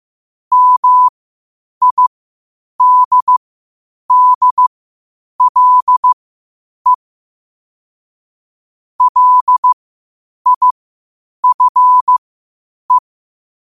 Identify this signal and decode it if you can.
Morse code, heard as a keyed tone, about 15 wpm, spaced as for 8 wpm, 1000 Hz. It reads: MIDDLE LIFE
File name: qotdfiles / 8wpm